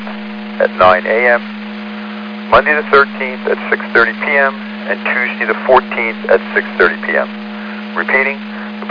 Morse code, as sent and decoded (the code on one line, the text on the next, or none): none